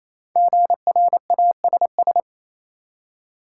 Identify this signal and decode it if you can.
Morse code, heard as a keyed tone, about 28 wpm, 700 Hz. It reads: GRASS
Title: Five Letter Words - 28wpm